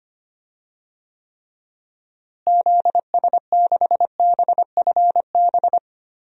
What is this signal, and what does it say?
Morse code, heard as a keyed tone, about 25 wpm, 700 Hz. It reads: ZS6BFB
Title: DXCC-100 Call Signs - 6 Characters - Repeated 2x - 25wpm